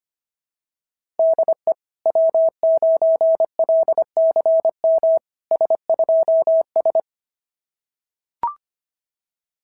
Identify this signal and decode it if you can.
Morse code, heard as a keyed tone, about 25 wpm, 690 Hz. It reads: DE W9LCM S2S
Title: SOTA QSO Practice 25wpm